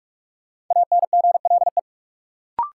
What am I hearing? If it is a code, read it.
Morse code, heard as a keyed tone, about 44 wpm, 700 Hz. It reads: ANGLE
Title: Sentences from Top 1500 Words - New Word Drill - 45wpm